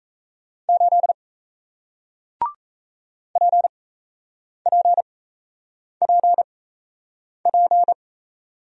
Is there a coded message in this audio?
Yes: Morse code from 0.7 to 7.9 s, heard as a keyed tone, about 38 wpm, spaced as for 8 wpm, 700 Hz. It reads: 8 PPPP